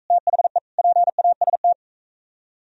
Morse code, heard as a keyed tone, about 45 wpm, 700 Hz. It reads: THE PAST